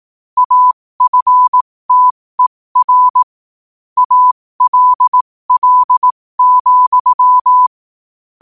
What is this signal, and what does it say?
Morse code, heard as a keyed tone, about 18 wpm, spaced as for 15 wpm, 1000 Hz. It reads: AFTER ALL,
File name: 15wpm